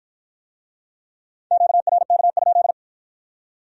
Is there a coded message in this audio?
Yes: Morse code from 1.5 to 2.7 s, heard as a keyed tone, about 51 wpm, 700 Hz. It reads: QRK?